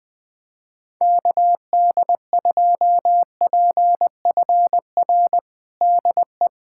Morse code, heard as a keyed tone, about 21 wpm, 700 Hz. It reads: KD2PFR DE
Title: USA Callsign DE Callsign - 6 Characters - 20wpm